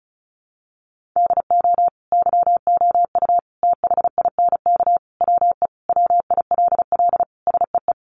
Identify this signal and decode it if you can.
Morse code, heard as a keyed tone, about 34 wpm, 700 Hz. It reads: DO YOU THINK WE WILL SEE
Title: Sentences from Top 1200 Words 35wpm